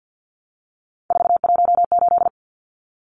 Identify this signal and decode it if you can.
Morse code, heard as a keyed tone, about 50 wpm, 700 Hz. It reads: 418
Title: Numbers - 3-Digit 50wpm